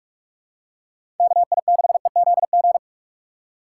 Morse code, heard as a keyed tone, about 46 wpm, 700 Hz. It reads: KI6EZG
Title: USA Call Signs - 6 Characters - 4x Character Spacing - 45wpm